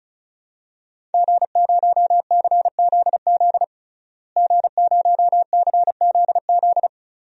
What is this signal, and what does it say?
Morse code, heard as a keyed tone, about 34 wpm, 700 Hz. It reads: G0CZZ G0CZZ
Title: DXCC-100 Call Signs - 5 Characters - Repeated 3x - 35wpm